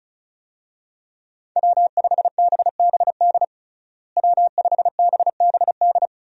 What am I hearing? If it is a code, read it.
Morse code, heard as a keyed tone, about 36 wpm, 700 Hz. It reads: W5BBD W5BBD
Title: USA Call Signs - 5 Characters - Repeated 2x - 35wpm